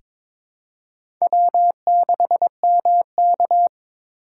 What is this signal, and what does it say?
Morse code, heard as a keyed tone, about 21 wpm, 700 Hz. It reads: W6MK
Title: USA Call Signs - 4 Characters - 2x Character Spacing - 22wpm